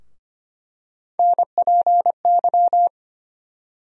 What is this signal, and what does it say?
Morse code, heard as a keyed tone, about 25 wpm, 700 Hz. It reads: NPY
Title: USA Call Sign Postfixes - 3 Letters 25wpm